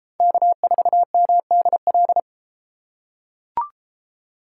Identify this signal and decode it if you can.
Morse code, heard as a keyed tone, about 30 wpm, 700 Hz. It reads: K4MDL